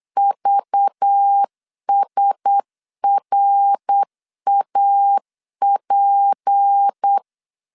Morse code, heard as a keyed tone, about 8 wpm, 800 Hz. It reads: VSRAP
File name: CW / DET / 1 9 1500C S C 8wpm 5653